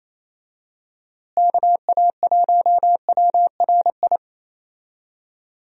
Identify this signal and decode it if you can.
Morse code, heard as a keyed tone, about 28 wpm, 700 Hz. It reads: KA1WRI
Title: USA Call Signs - 6 Characters - Repeated 2x - 28wpm